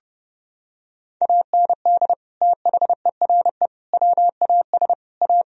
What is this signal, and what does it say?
Morse code, heard as a keyed tone, about 30 wpm, 700 Hz. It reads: AND THERE WAS A
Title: Sentences from Top 1000 Words - Review Part 3 - 30wpm